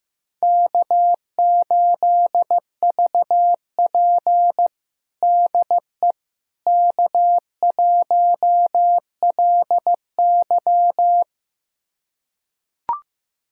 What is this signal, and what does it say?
Morse code, heard as a keyed tone, about 15 wpm, 700 Hz. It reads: K8VP DE K1LY
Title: USA Callsign DE Callsign - 4 Characters - 15wpm